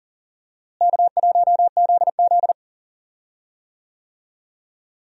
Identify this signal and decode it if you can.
Morse code, heard as a keyed tone, about 40 wpm, 700 Hz. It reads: K1ZZ
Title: USA Call Signs - 4 Characters - Repeated 3x - 40wpm